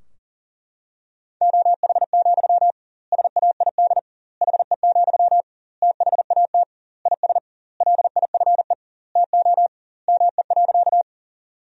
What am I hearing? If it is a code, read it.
Morse code, heard as a keyed tone, about 40 wpm, 700 Hz. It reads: OH, SAID HE, THAT IS LIFE TO ME.
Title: 2x Word-Spacing - Sentences from Top 500 Words - 40wpm